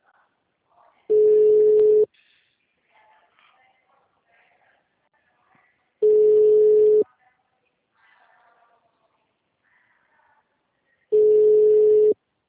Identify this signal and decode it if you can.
Ringback tone (Europe, 425 Hz).
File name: out- -2001- - - .81949